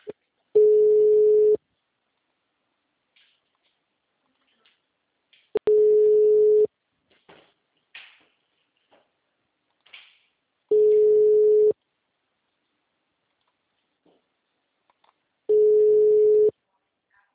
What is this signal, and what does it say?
Ringback tone (Europe, 425 Hz).
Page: Valinhos - SP